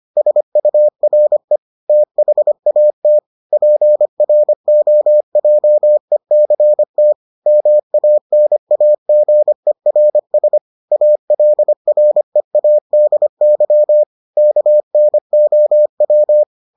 Morse code, heard as a keyed tone, about 25 wpm, 600 Hz. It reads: SURE THAT PROJECT MANAGERS ALREADY KNOW